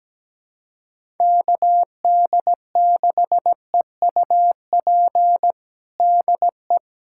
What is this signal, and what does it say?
Morse code, heard as a keyed tone, about 17 wpm, 700 Hz. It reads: KD6EUP DE
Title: USA Callsign DE Callsign - 6 Characters - 17wpm